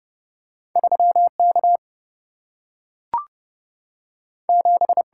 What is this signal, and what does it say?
Morse code, heard as a keyed tone, about 30 wpm, 700 Hz. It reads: 3 K 7
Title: Single-Letter Single-Number 30wpm